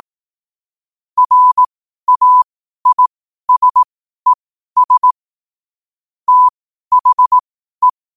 Morse code, heard as a keyed tone, about 18 wpm, spaced as for 12 wpm, 1000 Hz. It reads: RAISES THE
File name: qotdfiles / 12wpm